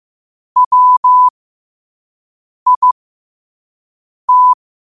Morse code, heard as a keyed tone, about 15 wpm, spaced as for 5 wpm, 1000 Hz. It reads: WIT
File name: qotdfiles / 5wpm